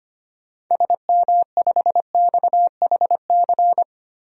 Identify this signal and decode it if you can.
Morse code, heard as a keyed tone, about 25 wpm, 700 Hz. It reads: SM5XHC
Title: DXCC-100 Call Signs - 6 Characters - 25wpm